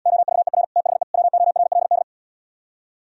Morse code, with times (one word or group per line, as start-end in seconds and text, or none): 0.1-0.6 OVER
0.8-1.0 THE
1.1-2.0 WORLD